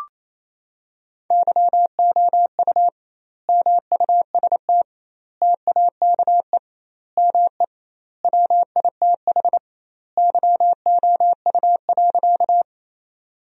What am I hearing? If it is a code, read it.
Morse code, heard as a keyed tone, about 28 wpm, 700 Hz. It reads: YOU MUST TAKE ME WITH YOU.
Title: 2x Word-Spacing - Sentences from Top 200 Words - 28wpm